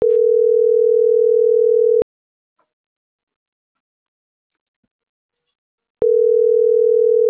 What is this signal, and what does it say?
Ringback tone (North America), 440 + 480 Hz.